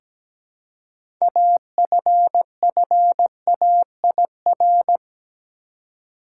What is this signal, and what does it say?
Morse code, heard as a keyed tone, about 17 wpm, 700 Hz. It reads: AFFAIR
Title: Six Letter Words - 17wpm